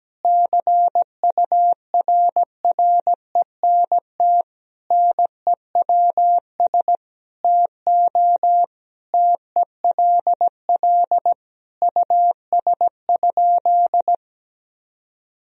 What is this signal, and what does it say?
Morse code, heard as a keyed tone, about 17 wpm, 700 Hz. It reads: CURRENT NEWS TO TELL US?
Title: Sentences from Top 1000 Words - Review Part 4 - 17wpm